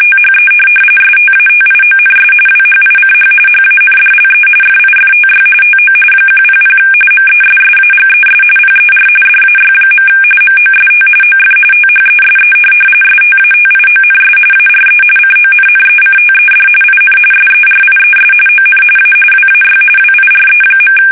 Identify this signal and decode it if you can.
RTTY, about 45.45 baud. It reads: THE STATIONS WHICH HANDLE FIFTY TO 100 MESSAGES PER DAY. MOST OF THESE STATIONS ARE CONNECTED IN SERIES ON A WAY WIRE.
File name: fsk850